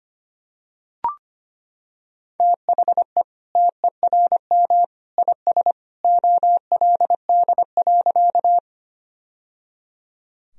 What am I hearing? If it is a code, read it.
Morse code, heard as a keyed tone, about 25 wpm, 700 Hz. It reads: THE TERM IS OLD.
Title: Sentences from Top 1000 Words - Review Part 4 - 25wpm